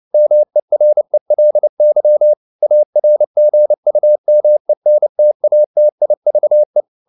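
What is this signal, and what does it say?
Morse code, heard as a keyed tone, about 29 wpm, 600 Hz. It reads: MERELY ARGUMENTATIVE
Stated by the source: Morse code at 29 wpm, sending Ken Follett's novel Never in English, from CW med Gnister.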